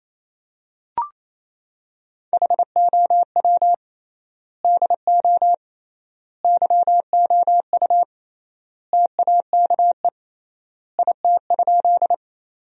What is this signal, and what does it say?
Morse code, heard as a keyed tone, about 28 wpm, 700 Hz. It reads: HOW DO YOU TAKE IT?
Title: 3x Word-Spacing - Sentences from Top 200 Words - 28wpm